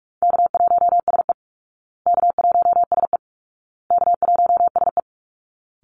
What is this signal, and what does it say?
Morse code, heard as a keyed tone, about 44 wpm, 700 Hz. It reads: K1SE K1SE K1SE
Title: Call Signs Repeated 2x - 45wpm